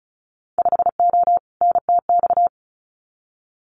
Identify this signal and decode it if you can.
Morse code, heard as a keyed tone, about 36 wpm, 700 Hz. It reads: 5O NTX
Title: Winter Field Day Exchanges Encore 35wpm